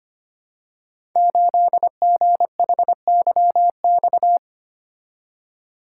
Morse code, heard as a keyed tone, about 25 wpm, 700 Hz. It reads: 8GHYX
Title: Code Groups - 5 Letter:Numbers - 25wpm